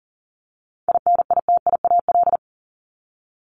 Morse code, heard as a keyed tone, about 40 wpm, 700 Hz. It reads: INITIAL
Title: Top 1501-1750 Words 40wpm